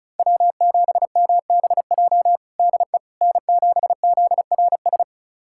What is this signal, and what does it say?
Morse code, heard as a keyed tone, about 36 wpm, 700 Hz. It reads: W7MBJ DE N7ZRS
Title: USA Callsign DE Callsign - 5 Characters - 35wpm